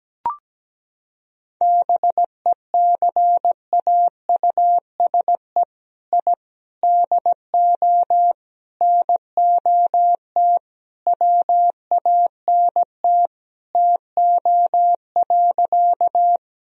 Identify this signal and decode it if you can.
Morse code, heard as a keyed tone, about 17 wpm, 700 Hz. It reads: BECAUSE I DO NOT WANT TO.
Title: Sentences from Top 1000 Words - Review Part 3 - 17wpm